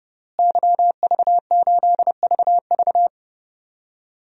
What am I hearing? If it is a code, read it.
Morse code, heard as a keyed tone, about 32 wpm, 700 Hz. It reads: YV8VV
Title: DXCC-100 Call Signs - 5 Characters - Repeated 2x - 30wpm